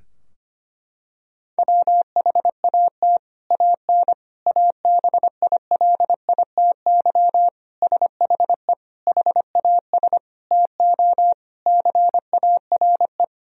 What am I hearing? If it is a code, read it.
Morse code, heard as a keyed tone, about 25 wpm, 700 Hz. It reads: WHAT AN ABILITY SHE HAS TO CARE
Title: Sentences from Top 1000 Words - Review Part 3 - 25wpm